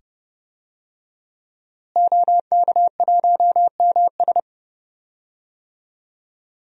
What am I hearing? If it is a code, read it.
Morse code, heard as a keyed tone, about 31 wpm, 700 Hz. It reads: OK1MS
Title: DXCC-100 Call Signs - 5 Characters - Repeated 2x - 30wpm